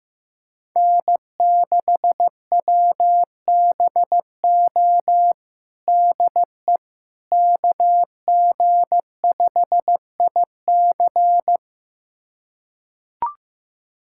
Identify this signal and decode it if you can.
Morse code, heard as a keyed tone, about 16 wpm, 700 Hz. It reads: N6WBO DE KG5IC
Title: USA Callsign DE Callsign - 5 Characters - 15wpm